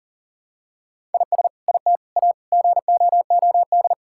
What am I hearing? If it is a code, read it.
Morse code, heard as a keyed tone, about 40 wpm, 700 Hz. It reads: IS IT A GOOD
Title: Sentences from Top 1000 Words - Review Part 5 - 40wpm